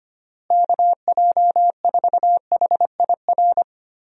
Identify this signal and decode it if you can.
Morse code, heard as a keyed tone, about 25 wpm, 700 Hz. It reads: KJ4HIR
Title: USA Call Signs - 6 Characters - Repeated 3x - 25wpm